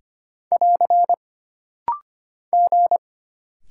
Morse code, heard as a keyed tone, about 25 wpm, 700 Hz. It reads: <AR> G
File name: Single Character (Letter:Number:Punctuation:Prosign) - Rapid-Fire - 25wpm